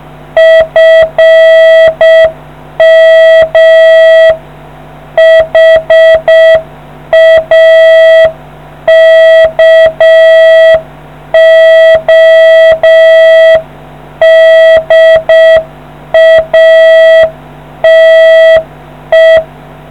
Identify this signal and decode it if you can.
Morse code, heard as a keyed tone, about 6 wpm, 650 Hz. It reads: FM HAKODATE